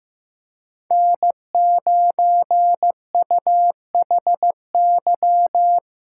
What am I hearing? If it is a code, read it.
Morse code, heard as a keyed tone, about 15 wpm, 700 Hz. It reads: N9UHY
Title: USA Call Signs - 5 Characters - 4x Character Spacing - 15wpm